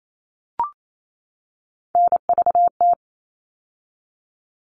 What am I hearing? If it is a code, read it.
Morse code, heard as a keyed tone, about 28 wpm, 700 Hz. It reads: NVT
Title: All Cut Numbers - 3-Digit 28wpm